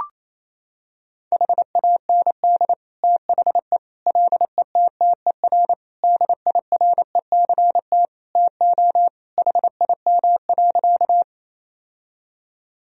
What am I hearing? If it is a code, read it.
Morse code, heard as a keyed tone, about 28 wpm, 700 Hz. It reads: HAND THE LETTER DIRECT TO HIM.
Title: Sentences from Top 1000 Words - Review Part 2 - 28wpm